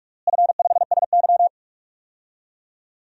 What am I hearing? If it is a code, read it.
Morse code, heard as a keyed tone, about 46 wpm, 700 Hz. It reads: F5SY
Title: DXCC-100 Call Signs - 4 Characters - Repeated 3x - 45wpm